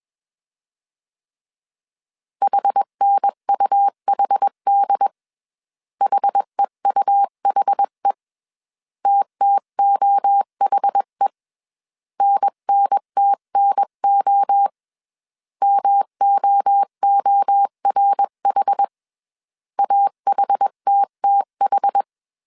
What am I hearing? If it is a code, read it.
Morse code, heard as a keyed tone, about 21 wpm, 800 Hz. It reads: HNUHD HEUHE TTOHE NNTNO MOORH AHTTH